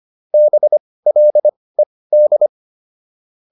Morse code, heard as a keyed tone, about 25 wpm, spaced as for 18 wpm, 600 Hz. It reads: BLED